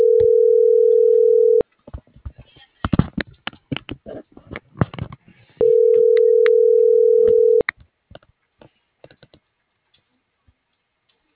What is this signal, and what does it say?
Ringback tone (North America), 440 + 480 Hz.